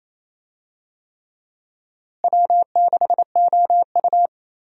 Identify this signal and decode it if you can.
Morse code, heard as a keyed tone, about 28 wpm, 700 Hz. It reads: W6OU
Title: USA Call Signs - 4 Characters - Repeated 3x - 28wpm